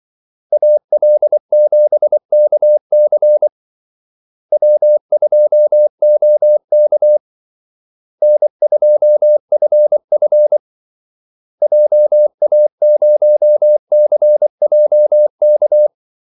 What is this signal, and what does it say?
Morse code, heard as a keyed tone, about 24 wpm, 600 Hz. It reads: AL7KC W2OK N2FF JA0CJK
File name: Callsigns-24wpm